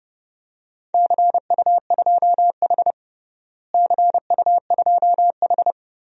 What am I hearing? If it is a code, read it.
Morse code, heard as a keyed tone, about 30 wpm, 700 Hz. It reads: CU2H CU2H
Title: DXCC-100 Call Signs - 4 Characters - Repeated 3x - 30wpm